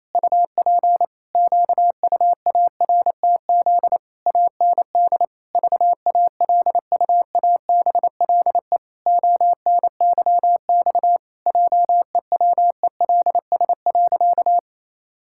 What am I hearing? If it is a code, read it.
Morse code, heard as a keyed tone, about 28 wpm, 700 Hz. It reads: UP QUARTZ AND VALUABLE ONYX JEWELS.
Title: Pangram Challenge v2 - 28wpm